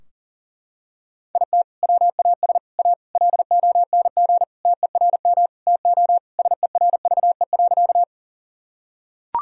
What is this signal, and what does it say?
Morse code, heard as a keyed tone, about 40 wpm, 700 Hz. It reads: IT WAS A LONG TERM TO SERVE.
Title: Sentences from Top 1000 Words - Review Part 4 - 40wpm